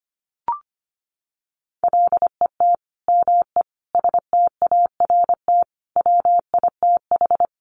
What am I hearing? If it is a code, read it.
Morse code, heard as a keyed tone, about 25 wpm, 700 Hz. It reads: LET ME START WITH
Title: Sets of 5 Words Encore 25wpm